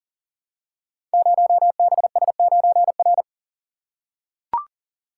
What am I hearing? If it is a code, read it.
Morse code, heard as a keyed tone, about 43 wpm, 700 Hz. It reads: 0BS9R
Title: Code Groups - 5 Letter:Numbers - 40wpm